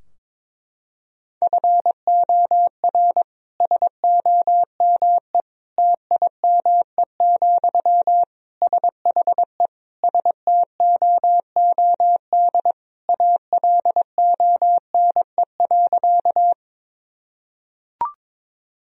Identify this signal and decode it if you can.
Morse code, heard as a keyed tone, about 22 wpm, 700 Hz. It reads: FOR SOME TIME, SHE STOOD ALONE.
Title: Sentences from Top 1000 Words - Review Part 2 - 22wpm